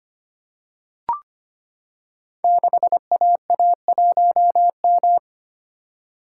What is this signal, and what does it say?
Morse code, heard as a keyed tone, about 25 wpm, 700 Hz. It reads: 6AA1M
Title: Code Groups - 5 Letter:Numbers - 25wpm